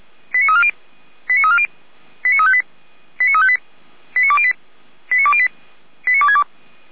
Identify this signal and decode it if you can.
Five-tone selective call (ZVEI), heard as quick runs of five tones, tones of about 70 ms. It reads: 89340 89340 89358 89358 89208 89208 89262